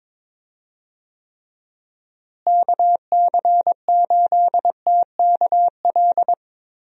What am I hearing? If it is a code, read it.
Morse code, heard as a keyed tone, about 22 wpm, 700 Hz. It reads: KC8TKL
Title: USA Call Signs - 6 Characters - Repeated 3x - 22wpm